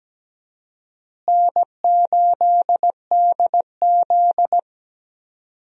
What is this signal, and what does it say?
Morse code, heard as a keyed tone, about 17 wpm, 700 Hz. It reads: N8DZ
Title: USA Call Signs - 4 Characters - Repeated 2x - 17wpm